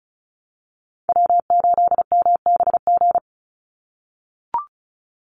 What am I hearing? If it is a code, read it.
Morse code, heard as a keyed tone, about 36 wpm, 700 Hz. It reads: W8MBG
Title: USA Call Signs 35wpm